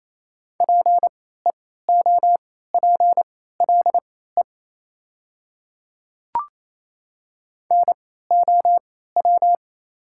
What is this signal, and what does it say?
Morse code, heard as a keyed tone, about 28 wpm, spaced as for 16 wpm, 690 Hz. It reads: PEOPLE NOW
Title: 3x Character Spacing - Top 100 Words - 28wpm